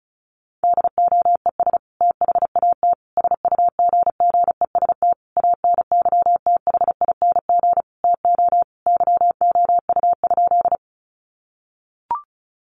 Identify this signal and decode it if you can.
Morse code, heard as a keyed tone, about 34 wpm, 700 Hz. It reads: DOES THAT SUGGEST ANYTHING TO YOU?
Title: Sentences from Top 1300 Words 35wpm